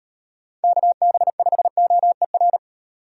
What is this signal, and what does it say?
Morse code, heard as a keyed tone, about 30 wpm, 700 Hz. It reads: KB5OER
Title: Speed Racing - USA 2x3 Call Signs 25wpm